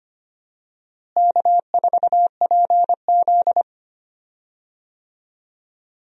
Morse code, heard as a keyed tone, about 25 wpm, 700 Hz. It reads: K4PZ
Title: USA Call Signs - 4 Characters - Repeated 3x - 25wpm